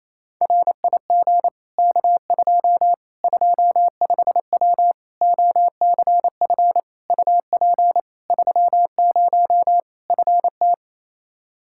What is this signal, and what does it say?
Morse code, heard as a keyed tone, about 28 wpm, 700 Hz. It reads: RIG K2 25W OCF UP 30 FT
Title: Practice QSO - Exchange 2 - Info - 28wpm